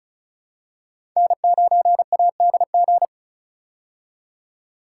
Morse code, heard as a keyed tone, about 36 wpm, 700 Hz. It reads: N9ADG
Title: K1USN SST QSO Practice - 35wpm